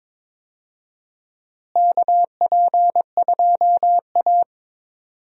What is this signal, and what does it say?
Morse code, heard as a keyed tone, about 21 wpm, 700 Hz. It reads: KP2A
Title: DXCC-100 Call Signs - 4 Characters - Repeated 3x - 22wpm